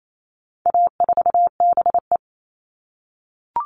Morse code, heard as a keyed tone, about 28 wpm, 700 Hz. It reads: A4BE
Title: All Cut Numbers - 4-Digit 28wpm